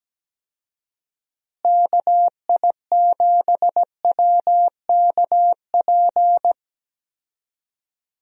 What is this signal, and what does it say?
Morse code, heard as a keyed tone, about 17 wpm, 700 Hz. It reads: KI7WKP
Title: USA Call Signs - 6 Characters - Repeated 2x - 17wpm